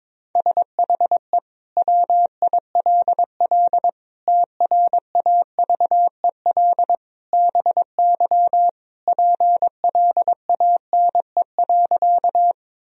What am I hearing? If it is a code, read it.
Morse code, heard as a keyed tone, about 22 wpm, 700 Hz. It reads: SHE WILL TRAVEL BY PLANE.
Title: Sentences from Top 1000 Words - Review Part 3 - 22wpm